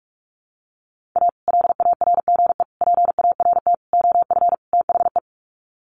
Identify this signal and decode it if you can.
Morse code, heard as a keyed tone, about 44 wpm, 700 Hz. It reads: A LARGE PART OF THE
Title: Sets of 5 Words Encore2 45wpm